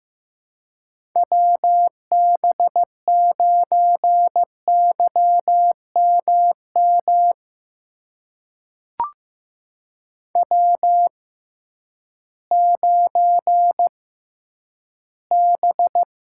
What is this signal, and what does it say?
Morse code, heard as a keyed tone, about 15 wpm, spaced as for 10 wpm, 700 Hz. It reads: WB9YMM W9B
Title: USA Call Signs - 6x Character Spacing - 15wpm